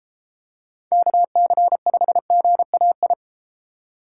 Morse code, heard as a keyed tone, about 32 wpm, 700 Hz. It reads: KC5GAI